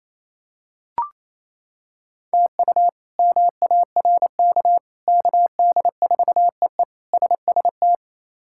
Morse code, heard as a keyed tone, about 27 wpm, 700 Hz. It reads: TU MARK KD4EE SST
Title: K1USN SST QSO Practice - 28wpm